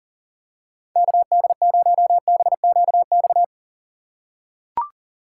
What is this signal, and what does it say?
Morse code, heard as a keyed tone, about 40 wpm, 700 Hz. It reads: KD0BQX
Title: USA Call Signs - 6 Characters - 6x Character Spacing - 40wpm